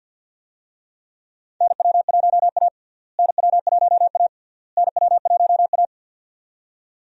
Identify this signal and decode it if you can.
Morse code, heard as a keyed tone, about 51 wpm, 700 Hz. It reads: NW1A NW1A NW1A
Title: USA Call Signs - 4 Characters - Repeated 3x - 50wpm